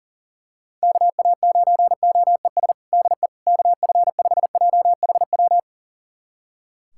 Morse code, heard as a keyed tone, about 40 wpm, 700 Hz. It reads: KA9OES DE KF5JHW
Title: USA Callsign DE Callsign 40wpm